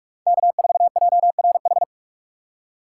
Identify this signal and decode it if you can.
Morse code, heard as a keyed tone, about 46 wpm, 700 Hz. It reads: K4JRH
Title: USA Call Signs - 3x Character Spacing - 45wpm